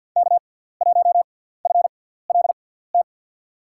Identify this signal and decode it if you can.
Morse code, heard as a keyed tone, about 52 wpm, spaced as for 17 wpm, 700 Hz. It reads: K1FLT
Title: USA Call Signs - 5 Characters - 6x Character Spacing - 50wpm